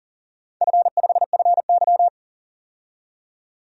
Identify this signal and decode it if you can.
Morse code, heard as a keyed tone, about 38 wpm, 700 Hz. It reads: F5FY
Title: DXCC-100 Call Signs - 4 Characters - 40wpm